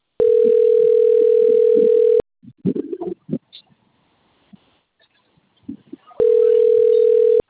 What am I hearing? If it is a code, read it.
Ringback tone (North America), 440 + 480 Hz.